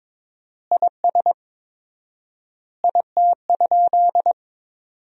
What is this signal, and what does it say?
Morse code, heard as a keyed tone, about 22 wpm, 700 Hz. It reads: IS IT?
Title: 4x Word-Spacing - Sentences from Top 100 Words - 22wpm